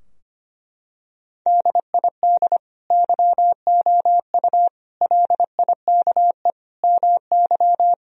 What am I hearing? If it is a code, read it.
Morse code, heard as a keyed tone, about 25 wpm, 700 Hz. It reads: DID YOU LIKE MY
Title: Sentences from Top 1000 Words - Review Part 2 - 25wpm